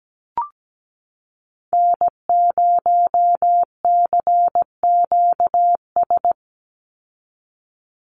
Morse code, heard as a keyed tone, about 17 wpm, 700 Hz. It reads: N0CQS
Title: USA Call Signs 17wpm